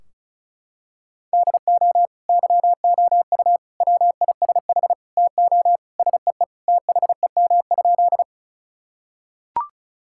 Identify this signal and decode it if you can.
Morse code, heard as a keyed tone, about 34 wpm, 700 Hz. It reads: DO YOU WISH TO SEE THEM?
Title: Sentences from Top 1600 Words 35wpm